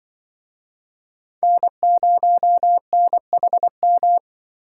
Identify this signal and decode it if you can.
Morse code, heard as a keyed tone, about 25 wpm, 700 Hz. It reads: N0NHM
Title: Speed Racing - USA 1x3 Call Signs 20wpm